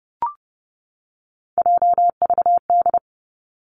Morse code, heard as a keyed tone, about 31 wpm, 700 Hz. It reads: JVD
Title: Three Challenging Letter-Number Combinations 30wpm